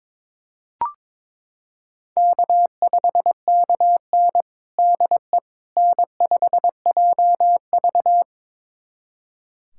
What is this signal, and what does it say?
Morse code, heard as a keyed tone, about 22 wpm, 700 Hz. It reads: K5KN DE N5JV
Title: USA Callsign DE Callsign - 4 Characters - 22wpm